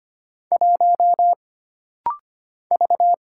Morse code, heard as a keyed tone, about 25 wpm, 700 Hz. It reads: 1 V
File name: Single Character (Letter:Number:Punctuation:Prosign) - Rapid-Fire - 25wpm